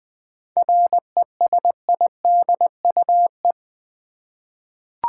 Morse code, heard as a keyed tone, about 21 wpm, 700 Hz. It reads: RESIDUE
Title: 7 Letter Words - First 3 Letters Given - 20wpm